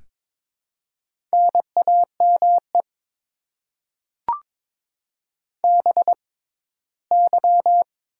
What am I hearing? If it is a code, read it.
Morse code, heard as a keyed tone, about 22 wpm, spaced as for 15 wpm, 700 Hz. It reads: NAME BY
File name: Top 90 QSO Elements - 6x Character Spacing - 22wpm